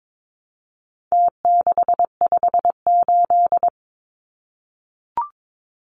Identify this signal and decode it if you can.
Morse code, heard as a keyed tone, about 22 wpm, 700 Hz. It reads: T658
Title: Numbers - Contest Style - 4-Digit 22wpm